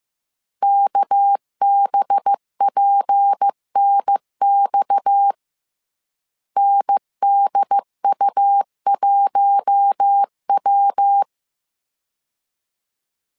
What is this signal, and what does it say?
Morse code, heard as a keyed tone, about 15 wpm, 800 Hz. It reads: KBPNX NDU1W